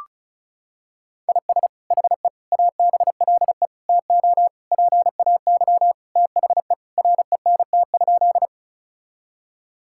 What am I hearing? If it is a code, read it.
Morse code, heard as a keyed tone, about 36 wpm, 700 Hz. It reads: IS HE ABLE TO PAY THE RENT?
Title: Sentences from Top 1000 Words - Review Part 5 - 35wpm